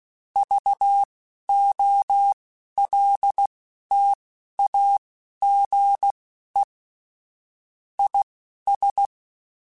Morse code, heard as a keyed tone, about 16 wpm, spaced as for 12 wpm, 800 Hz. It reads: VOLTAGE IS
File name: Technical Subjects / Technical Subjects-mp3s / XL @10